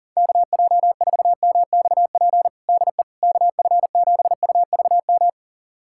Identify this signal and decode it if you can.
Morse code, heard as a keyed tone, about 40 wpm, 700 Hz. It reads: KJ4MXP DE KF7UVM
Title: USA Callsign DE Callsign 40wpm